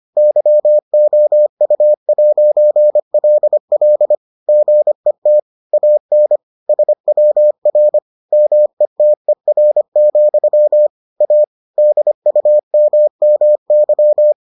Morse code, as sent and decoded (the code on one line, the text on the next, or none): -.-- --- ..- .----. .-.. .-.. / --. - / .- -. / ... .-- .-. / -- . - . .-. --..-- / .- / -.. ..- -- -- -.--
YOU'LL GET AN SWR METER, A DUMMY